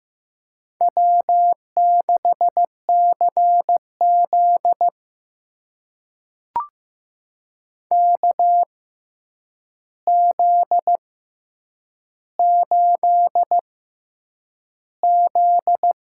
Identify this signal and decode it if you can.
Morse code, heard as a keyed tone, about 15 wpm, spaced as for 8 wpm, 700 Hz. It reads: W6CZ KZ8Z
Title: USA Call Signs - 6x Character Spacing - 15wpm